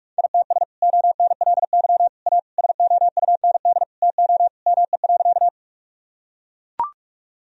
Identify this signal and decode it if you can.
Morse code, heard as a keyed tone, about 41 wpm, 700 Hz. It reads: ITS ONLY A SOUND TO ME.
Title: Sentences from Top 1000 Words - Review Part 1 - 45wpm